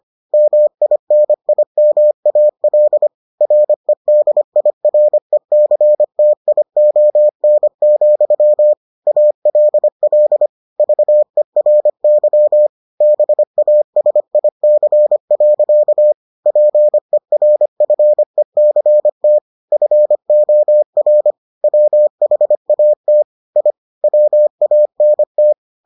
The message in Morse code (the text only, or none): MINIMAL REDIRECTION, ALL VERY BASIC. PERFECT FOR WHAT I WANT